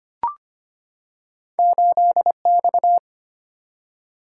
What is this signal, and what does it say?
Morse code, heard as a keyed tone, about 25 wpm, 700 Hz. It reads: 8X
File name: Two Letter-Number Combinations - 25wpm